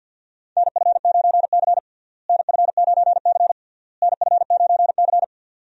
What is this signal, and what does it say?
Morse code, heard as a keyed tone, about 52 wpm, 700 Hz. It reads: NF9C NF9C NF9C
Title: USA Call Signs - 4 Characters - Repeated 3x - 50wpm